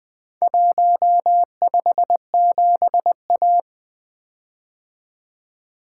Morse code, heard as a keyed tone, about 23 wpm, 700 Hz. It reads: 157A
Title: Speed-Racing - Four Letter-Number Combinations - 20wpm